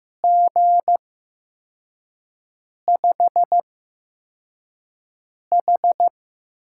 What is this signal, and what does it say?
Morse code, heard as a keyed tone, about 15 wpm, spaced as for 4 wpm, 700 Hz. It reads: G5H
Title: USA Call Signs - 6 Characters - 8x Character Spacing - 15wpm